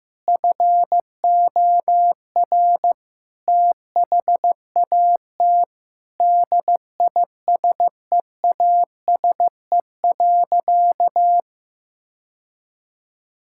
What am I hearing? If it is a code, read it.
Morse code, heard as a keyed tone, about 15 wpm, 700 Hz. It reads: FOR THAT DISEASE.
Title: Sentences from Top 1000 Words - Review Part 6 - 15wpm